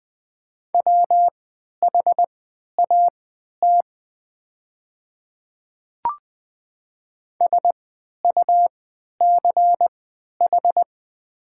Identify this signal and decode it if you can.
Morse code, heard as a keyed tone, about 20 wpm, spaced as for 11 wpm, 690 Hz. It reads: WHAT SUCH
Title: 3x Character Spacing - Top 200 Words - 20wpm